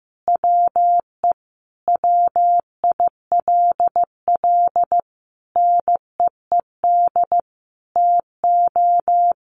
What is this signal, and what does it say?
Morse code, heard as a keyed tone, about 15 wpm, 700 Hz. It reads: WE WILL NEED TO
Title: Sentences from Top 1500 Words 15wpm